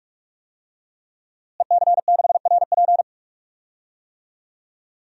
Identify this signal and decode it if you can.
Morse code, heard as a keyed tone, about 43 wpm, 700 Hz. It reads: EC6RP
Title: DXCC-100 Call Signs - 5 Characters - Repeated 3x - 45wpm